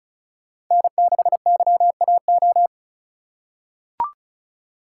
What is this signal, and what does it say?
Morse code, heard as a keyed tone, about 36 wpm, 690 Hz. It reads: N6YAO
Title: USA Call Signs - 5 Characters - 3x Character Spacing - 35wpm